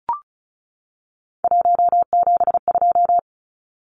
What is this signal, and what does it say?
Morse code, heard as a keyed tone, about 34 wpm, 700 Hz. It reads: 172
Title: Numbers - 3-Digit 35wpm